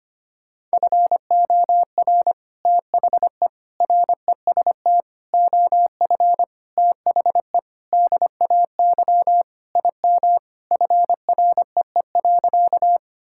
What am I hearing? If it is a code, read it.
Morse code, heard as a keyed tone, about 25 wpm, 700 Hz. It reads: FOR THE REST OF THE DAY IM FREE.
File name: practice / 25wpm / Sentences with 3-Word Phrases - 25wpm